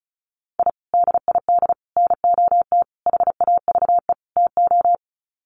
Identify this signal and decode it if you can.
Morse code, heard as a keyed tone, about 34 wpm, 700 Hz. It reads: I DID NOT HAVE TO
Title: Sets of 5 Words 35wpm